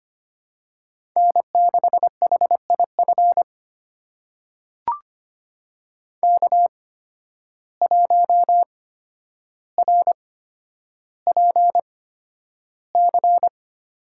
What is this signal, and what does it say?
Morse code, heard as a keyed tone, about 25 wpm, spaced as for 11 wpm, 700 Hz. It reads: N6HIF K1RPC
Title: USA Call Signs - 5 Characters - 8x Character Spacing - 25wpm